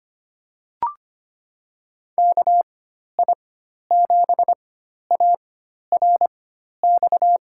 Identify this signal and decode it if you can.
Morse code, heard as a keyed tone, about 25 wpm, spaced as for 12 wpm, 700 Hz. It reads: KI7ARX
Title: USA Call Signs - 6 Characters - 4x Character Spacing - 25wpm